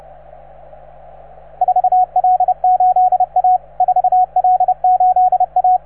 Morse code, heard as a keyed tone, about 32 wpm, 750 Hz, with radio noise. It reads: 4L8A 4L8A